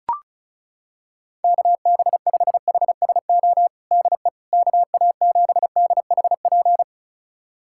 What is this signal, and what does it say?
Morse code, heard as a keyed tone, about 36 wpm, 700 Hz. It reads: KB5HSO DE KA7DHP
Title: USA Callsign DE Callsign 35wpm